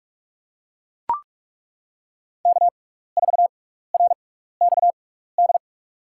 Morse code, heard as a keyed tone, about 42 wpm, spaced as for 15 wpm, 700 Hz. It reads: K4RXD
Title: USA Call Signs - 5 Characters - 6x Character Spacing - 45wpm